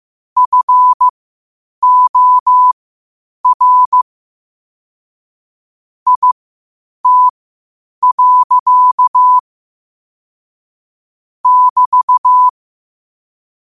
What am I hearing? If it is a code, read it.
Morse code, heard as a keyed tone, about 15 wpm, spaced as for 8 wpm, 1000 Hz. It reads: FOR IT. <BT>
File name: qotdfiles / 8wpm